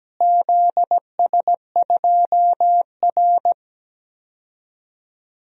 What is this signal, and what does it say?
Morse code, heard as a keyed tone, about 17 wpm, 700 Hz. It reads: ZS2R
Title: DXCC-100 Call Signs - 4 Characters - Repeated 3x - 17wpm